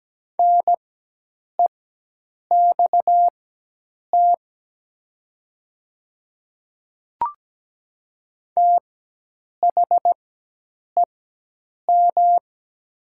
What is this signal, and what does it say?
Morse code, heard as a keyed tone, about 17 wpm, spaced as for 8 wpm, 700 Hz. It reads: NEXT THEM
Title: 4x Character Spacing - Top 300 Words - 17wpm